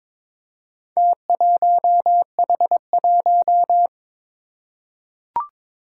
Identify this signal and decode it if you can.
Morse code, heard as a keyed tone, about 21 wpm, 700 Hz. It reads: T1H1
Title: Four Challenging Letter-Number Combinations 22wpm